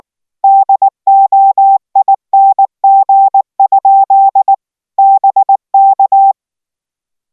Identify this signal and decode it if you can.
Morse code, heard as a keyed tone, about 19 wpm, 780 Hz. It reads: DOING? BK